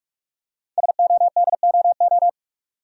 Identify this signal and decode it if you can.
Morse code, heard as a keyed tone, about 44 wpm, 700 Hz. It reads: SODOO
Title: Code Groups - 5 Letters - 45wpm